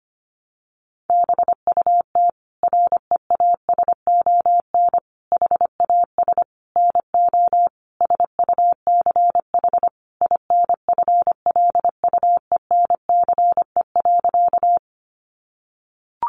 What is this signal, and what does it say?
Morse code, heard as a keyed tone, about 25 wpm, 700 Hz. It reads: BUT REASON HAS NO SUCH INFLUENCE.
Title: Sentences from Top 1100 Words 25wpm